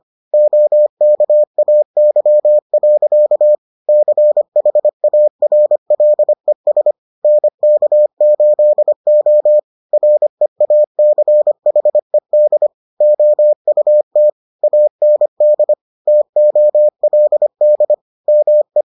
Morse code, heard as a keyed tone, about 25 wpm, 600 Hz. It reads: OKAY. CHARLES NK8O REACHED OUT AND TOLD ME